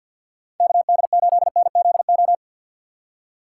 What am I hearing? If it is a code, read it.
Morse code, heard as a keyed tone, about 47 wpm, 700 Hz. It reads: KD8NZO
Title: USA Call Signs - 6 Characters - 6x Character Spacing - 50wpm